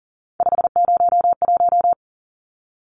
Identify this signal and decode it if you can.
Morse code, heard as a keyed tone, about 40 wpm, 700 Hz. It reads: 501
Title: Numbers - 3-Digit 40wpm